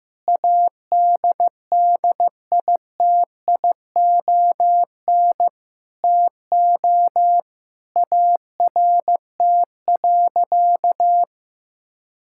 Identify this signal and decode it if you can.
Morse code, heard as a keyed tone, about 16 wpm, 700 Hz. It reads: ADDITION TO ART.
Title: Sentences from Top 1000 Words - Review Part 6 - 15wpm